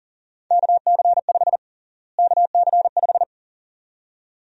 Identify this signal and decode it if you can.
Morse code, heard as a keyed tone, about 40 wpm, 700 Hz. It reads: KC5 KC5
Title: USA Call Sign Prefixes - Repeated 2x - 40wpm